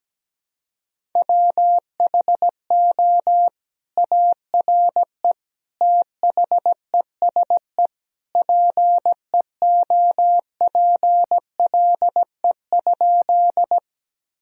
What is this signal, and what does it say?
Morse code, heard as a keyed tone, about 17 wpm, 700 Hz. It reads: WHO ARE THESE PEOPLE?
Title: Sentences from Top 1000 Words - Review Part 1 - 17wpm